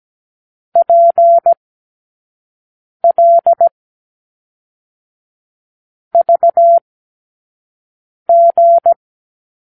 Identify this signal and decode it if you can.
Morse code, heard as a keyed tone, about 17 wpm, 680 Hz. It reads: P L V G
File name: k7qo / Lesson 36 (A-X)